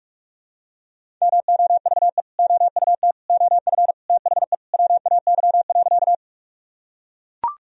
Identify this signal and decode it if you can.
Morse code, heard as a keyed tone, about 41 wpm, 700 Hz. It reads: MOVE OUT OF THE WAY.
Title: Sentences from Top 1000 Words - Review Part 1 - 45wpm